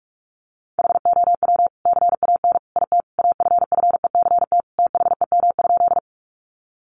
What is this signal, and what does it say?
Morse code, heard as a keyed tone, about 44 wpm, 700 Hz. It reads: HOW CAN IT AFFECT THEM?
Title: Sentences from Top 900 Words 45wpm